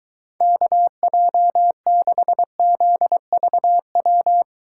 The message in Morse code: KJ6ZVW

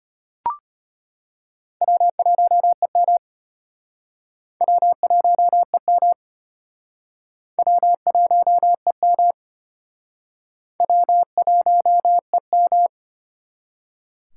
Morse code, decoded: W1EM W1EM W1EM W1EM

